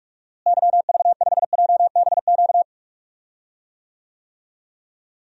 Morse code, heard as a keyed tone, about 46 wpm, 700 Hz. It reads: YV5JBQ
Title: DXCC-100 Call Signs - 6 Characters - Repeated 2x - 45wpm